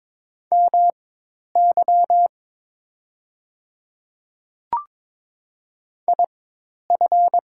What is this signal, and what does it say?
Morse code, heard as a keyed tone, about 22 wpm, spaced as for 10 wpm, 700 Hz. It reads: MY IF
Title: 4x Character Spacing - Top 100 Words - 22wpm